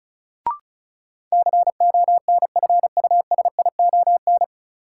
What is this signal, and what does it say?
Morse code, heard as a keyed tone, about 34 wpm, 700 Hz. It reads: CONFUSION
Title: Sentences from Top 1600 Words - New Word Drill Alt - 35wpm